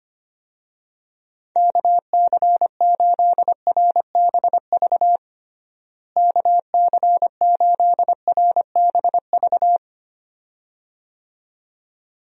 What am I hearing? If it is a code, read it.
Morse code, heard as a keyed tone, about 25 wpm, 700 Hz. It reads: KC8RBV KC8RBV
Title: USA Call Signs - 6 Characters - Repeated 2x - 25wpm